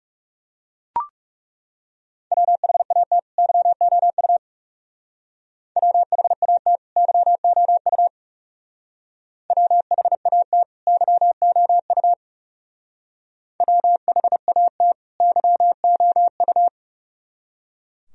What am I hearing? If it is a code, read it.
Morse code, heard as a keyed tone, about 37 wpm, 700 Hz. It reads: WHAT YOU WHAT YOU WHAT YOU WHAT YOU
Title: Speed-Racing - Sets of 2 Words 30wpm